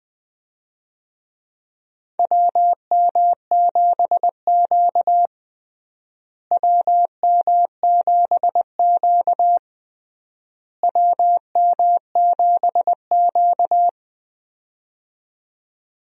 Morse code, heard as a keyed tone, about 20 wpm, 700 Hz. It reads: WM7Q WM7Q WM7Q
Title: USA Call Signs - 4 Characters - Repeated 3x - 20wpm